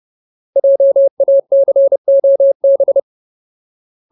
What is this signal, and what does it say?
Morse code, heard as a keyed tone, about 30 wpm, 550 Hz. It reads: JACOB
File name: Ultimate Word List / 30WPM Ultimate